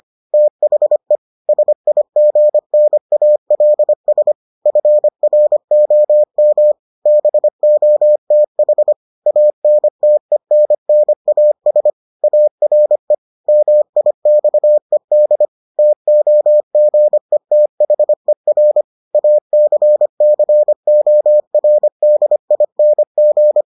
Morse code, heard as a keyed tone, about 25 wpm, 600 Hz. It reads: THE SIGNALS FROM BOTH ANTENNAS ARE MIXED TOGETHER ACCORDING